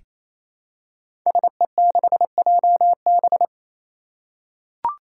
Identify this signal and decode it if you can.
Morse code, heard as a keyed tone, about 29 wpm, 700 Hz. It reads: SE6JB
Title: Five Challenging Letter-Number Combinations 28wpm